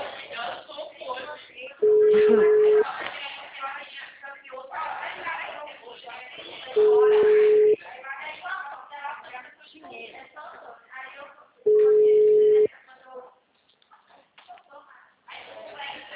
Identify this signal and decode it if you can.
Ringback tone (Europe, 425 Hz), with speech.